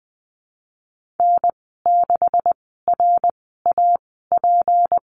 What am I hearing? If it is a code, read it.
Morse code, heard as a keyed tone, about 20 wpm, spaced as for 15 wpm, 700 Hz. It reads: N6RAP
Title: 154 Lesson 25g - Letter p - Call Signs 20wpm